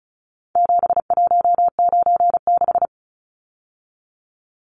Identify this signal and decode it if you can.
Morse code, heard as a keyed tone, about 34 wpm, 700 Hz. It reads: 7196
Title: Numbers - 4-Digit 35wpm